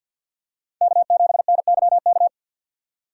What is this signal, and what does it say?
Morse code, heard as a keyed tone, about 47 wpm, 700 Hz. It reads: K7NYK
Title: USA Call Signs - 2x Character Spacing - 50wpm